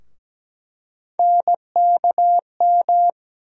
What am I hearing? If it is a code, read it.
Morse code, heard as a keyed tone, about 17 wpm, 700 Hz. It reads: NKM
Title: USA Call Sign Postfixes - 3 Letters 17wpm